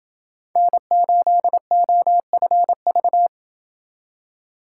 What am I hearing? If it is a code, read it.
Morse code, heard as a keyed tone, about 28 wpm, 700 Hz. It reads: N8OFV